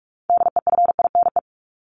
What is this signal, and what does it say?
Morse code, heard as a keyed tone, about 44 wpm, 700 Hz. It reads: DEFINE
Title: ED Suffix - Top 100 Words with Base Words 45wpm